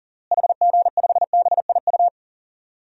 Morse code, heard as a keyed tone, about 40 wpm, 700 Hz. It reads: HG5BIU
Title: DXCC-100 Call Signs - 6 Characters - Repeated 3x - 40wpm